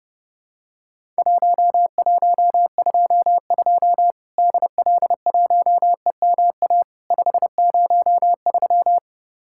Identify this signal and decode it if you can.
Morse code, heard as a keyed tone, about 30 wpm, 700 Hz. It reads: 1122 DL1EMA 503
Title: Worked All Europe DX Contest - QTC Exchange - 30wpm